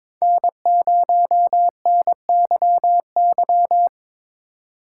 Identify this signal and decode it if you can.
Morse code, heard as a keyed tone, about 23 wpm, 700 Hz. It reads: N0NYY
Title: USA Call Signs - 5 Characters - 4x Character Spacing - 22wpm